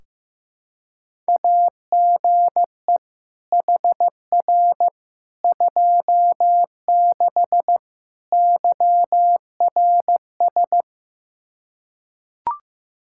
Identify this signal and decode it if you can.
Morse code, heard as a keyed tone, about 15 wpm, 700 Hz. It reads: AGE HR 26 YRS
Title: Practice QSO - Exchange 3 - More Info - 15wpm